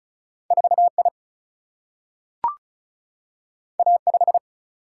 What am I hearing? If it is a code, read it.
Morse code, heard as a keyed tone, about 36 wpm, 700 Hz. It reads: 4I A5
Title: Single-Letter Single-Number 35wpm